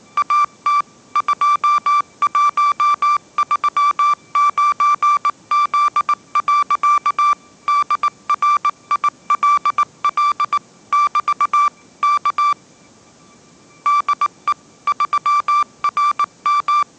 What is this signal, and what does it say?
Morse code, heard as a keyed tone, about 19 wpm, 1170 Hz. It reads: AT 2139Z. DRILL <BT> K DE 3RM
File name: 20wpm